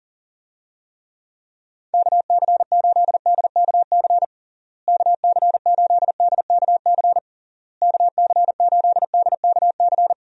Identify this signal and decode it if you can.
Morse code, heard as a keyed tone, about 40 wpm, 700 Hz. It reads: KC8DKC KC8DKC KC8DKC
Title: USA Call Signs - 6 Characters - Repeated 3x - 40wpm